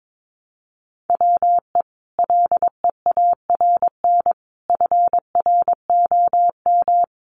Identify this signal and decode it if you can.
Morse code, heard as a keyed tone, about 22 wpm, 700 Hz. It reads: WE LEARN FROM
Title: Sentences from Top 900 Words 22wpm